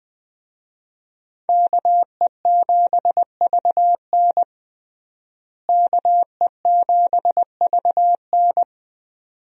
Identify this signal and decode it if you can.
Morse code, heard as a keyed tone, about 21 wpm, 700 Hz. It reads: KE7VN KE7VN
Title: USA Call Signs - 5 Characters - Repeated 2x - 20wpm